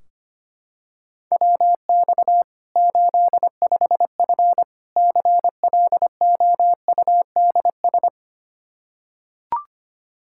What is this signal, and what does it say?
Morse code, heard as a keyed tone, about 25 wpm, 700 Hz. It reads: WX 85F CLOUDS
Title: Practice QSO - Exchange 2 - Info - 25wpm